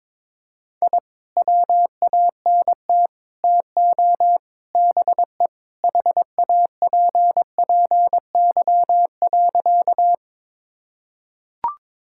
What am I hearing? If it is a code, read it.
Morse code, heard as a keyed tone, about 22 wpm, 700 Hz. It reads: I WANT TO BE HAPPY.
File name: practice / 22wpm / Sentences with 4-Word Phrases - 22wpm